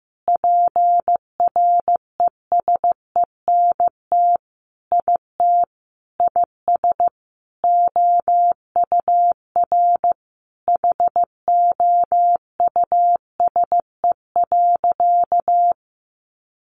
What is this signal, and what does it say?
Morse code, heard as a keyed tone, about 15 wpm, 700 Hz. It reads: PRESENT IT IS OUR HOUSE.
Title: Sentences from Top 800 Words 15wpm